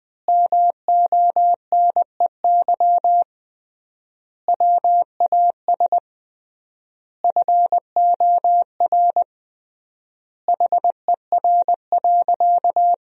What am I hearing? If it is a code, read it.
Morse code, heard as a keyed tone, about 21 wpm, 700 Hz. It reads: MONEY WAS FOR HER.
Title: 3x Word-Spacing - Sentences from Top 400 Words - 20wpm